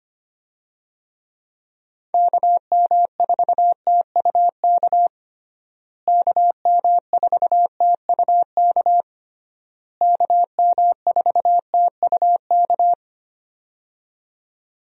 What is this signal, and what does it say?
Morse code, heard as a keyed tone, about 25 wpm, 700 Hz. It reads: KM4TUK KM4TUK KM4TUK
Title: USA Call Signs - 6 Characters - Repeated 3x - 25wpm